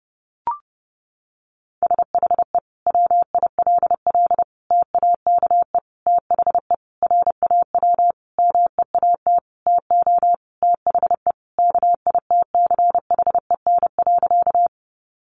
Morse code, heard as a keyed tone, about 30 wpm, 700 Hz. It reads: SHE WILL TAKE THE RAW MEAT TO THE KITCHEN.
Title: Sentences from Top 1500 Words 30wpm